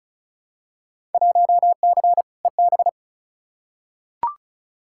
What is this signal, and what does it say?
Morse code, heard as a keyed tone, about 34 wpm, 700 Hz. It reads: 1C EB
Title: Field Day Exchanges Encore - 35wpm